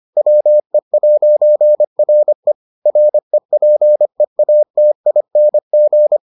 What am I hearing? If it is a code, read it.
Morse code, heard as a keyed tone, about 25 wpm, 600 Hz. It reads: WE'RE REPEATING